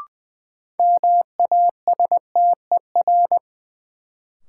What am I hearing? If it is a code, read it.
Morse code, heard as a keyed tone, about 20 wpm, 700 Hz. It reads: MASTER
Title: Sentences from Top 1300 Words - New Word Drill - 20wpm